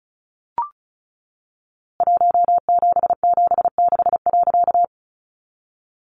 Morse code, heard as a keyed tone, about 34 wpm, 700 Hz. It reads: 1776.
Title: A Sound Of Thunder 35wpm